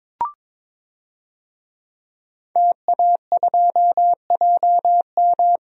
Morse code, heard as a keyed tone, about 22 wpm, 700 Hz. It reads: TA2JM
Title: DXCC-100 Call Signs - 5 Characters - 22wpm